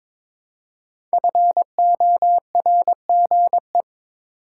Morse code, heard as a keyed tone, about 22 wpm, 700 Hz. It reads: FORGE
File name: 5 Letter Words - First 3 Letters Given - 22wpm